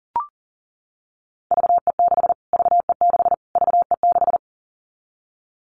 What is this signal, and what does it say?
Morse code, heard as a keyed tone, about 40 wpm, 700 Hz. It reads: VE6 VE6 VE6
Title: Callsign Prefixes for Top 25 DXCC - Repeated 3x - 40wpm